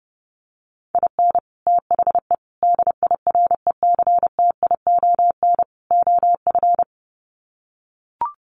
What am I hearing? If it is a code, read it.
Morse code, heard as a keyed tone, about 28 wpm, 700 Hz. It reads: IN THE DIRECTION OF
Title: Sets of 4 Words 30wpm